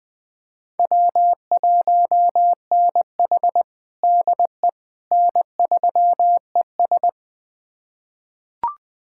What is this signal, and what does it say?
Morse code, heard as a keyed tone, about 20 wpm, 700 Hz. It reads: W1NH DE N3ES
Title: USA Callsign DE Callsign - 4 Characters - 20wpm